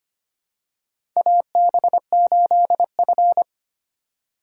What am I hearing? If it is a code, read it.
Morse code, heard as a keyed tone, about 25 wpm, 700 Hz. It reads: AB8F
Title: USA Call Signs - 4 Characters - 2x Character Spacing - 25wpm